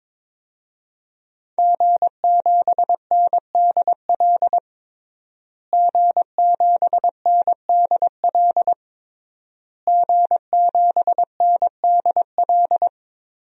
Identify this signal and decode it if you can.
Morse code, heard as a keyed tone, about 22 wpm, 700 Hz. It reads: G7NDL G7NDL G7NDL
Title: DXCC-100 Call Signs - 5 Characters - Repeated 3x - 22wpm